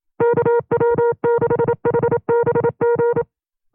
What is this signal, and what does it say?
Morse code, heard as a keyed tone, about 27 wpm, 470 Hz. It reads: KW6HBG
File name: 25WPM